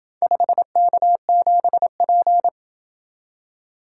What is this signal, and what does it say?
Morse code, heard as a keyed tone, about 26 wpm, 700 Hz. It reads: 5K7P